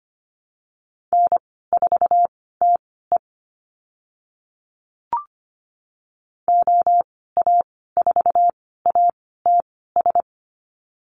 Morse code, heard as a keyed tone, about 25 wpm, spaced as for 16 wpm, 700 Hz. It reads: N4TE OA4ATS
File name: Lesson 15g - Number 5 - Call Signs 25wpm